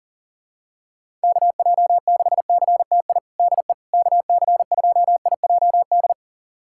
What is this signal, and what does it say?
Morse code, heard as a keyed tone, about 40 wpm, 700 Hz. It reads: KJ6CTI DE KC2IJD
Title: USA Callsign DE Callsign - 6 Characters - 40wpm